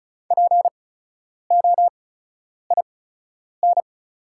Morse code, heard as a keyed tone, about 35 wpm, spaced as for 10 wpm, 700 Hz. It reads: POIN
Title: 8x Character Spacing - Top 300 Words - 35wpm